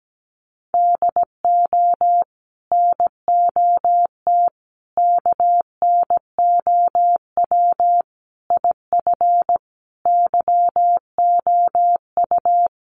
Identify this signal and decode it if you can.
Morse code, heard as a keyed tone, about 17 wpm, 700 Hz. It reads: DO NOT KNOW IF YOU
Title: Sets of 5 Words 17wpm